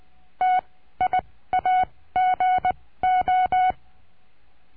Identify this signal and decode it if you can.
Morse code, heard as a keyed tone, about 20 wpm, spaced as for 15 wpm, 740 Hz. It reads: TIAGO
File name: CURSO DE TELEGRAFIA